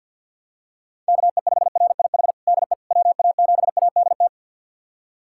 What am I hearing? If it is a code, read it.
Morse code, heard as a keyed tone, about 52 wpm, 700 Hz. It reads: KE5RIH DE WA7ADT
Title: USA Callsign DE Callsign - 6 Characters - 50wpm